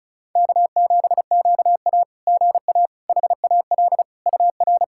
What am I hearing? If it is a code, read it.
Morse code, heard as a keyed tone, about 35 wpm, 700 Hz. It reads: K7QA GA HAL UR
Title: SOTA QSO Practice 35wpm